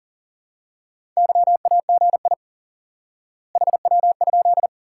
Morse code, heard as a keyed tone, about 40 wpm, 700 Hz. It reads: YAGI HW?
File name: course / instant-qso-element-course-3 / IQE Course 3 - Lesson 89 - YAGI - Two Elements - 40wpm